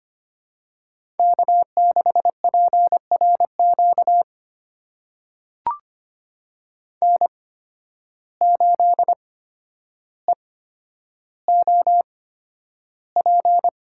Morse code, heard as a keyed tone, about 25 wpm, spaced as for 11 wpm, 700 Hz. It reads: K6PRQ N8EOP
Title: USA Call Signs - 5 Characters - 8x Character Spacing - 25wpm